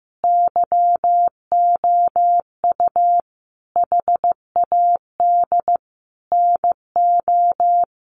Morse code, heard as a keyed tone, about 15 wpm, 700 Hz. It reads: YOU HAD NO
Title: Sentences from Top 1400 Words 15wpm